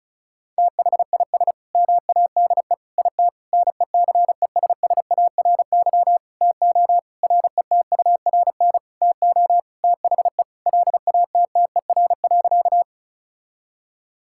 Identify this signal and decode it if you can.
Morse code, heard as a keyed tone, about 34 wpm, 700 Hz. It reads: THIS MADE IT NECESSARY TO RETURN TO THE LATTER.
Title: Sentences from Top 1600 Words 35wpm